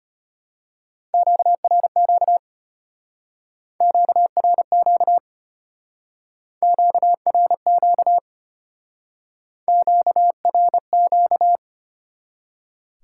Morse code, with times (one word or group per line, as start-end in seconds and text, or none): 1.1-2.4 QRQ
3.8-5.2 QRQ
6.6-8.2 QRQ
9.7-11.6 QRQ